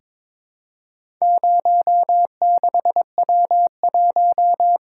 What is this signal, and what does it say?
Morse code, heard as a keyed tone, about 23 wpm, 700 Hz. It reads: 06W1
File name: practice / 22wpm / Four Letter-Number Combinations - 22wpm